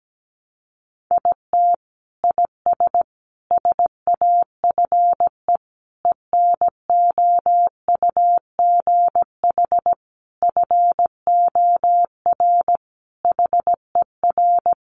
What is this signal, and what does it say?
Morse code, heard as a keyed tone, about 17 wpm, 700 Hz. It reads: IT IS SAFE ENOUGH FOR HER
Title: Sentences from Top 900 Words 17wpm